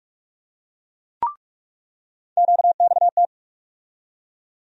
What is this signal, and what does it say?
Morse code, heard as a keyed tone, about 43 wpm, 700 Hz. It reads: QXT
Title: USA Call Sign Postfixes - 3 Letters 45wpm